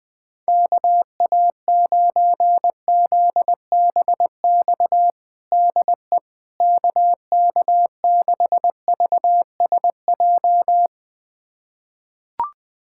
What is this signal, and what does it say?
Morse code, heard as a keyed tone, about 20 wpm, 700 Hz. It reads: KA9ZBX DE KK6VSJ
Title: USA Callsign DE Callsign - 6 Characters - 20wpm